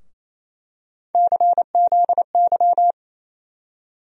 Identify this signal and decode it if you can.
Morse code, heard as a keyed tone, about 28 wpm, 700 Hz. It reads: CZY
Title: USA Call Sign Postfixes - 3 Letters 28wpm